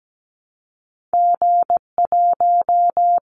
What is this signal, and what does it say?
Morse code, heard as a keyed tone, about 17 wpm, 700 Hz. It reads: G1
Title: Callsign Prefixes for Top 25 DXCC - Repeated 3x - 17wpm